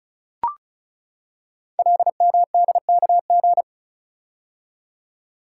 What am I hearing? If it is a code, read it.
Morse code, heard as a keyed tone, about 36 wpm, 700 Hz. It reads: LMDKG
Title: Code Groups - 5 Letters - 35wpm